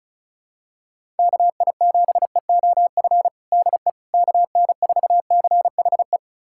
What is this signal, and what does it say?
Morse code, heard as a keyed tone, about 34 wpm, 700 Hz. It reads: KI7EOF DE KN4CHE
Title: USA Callsign DE Callsign - 6 Characters - 35wpm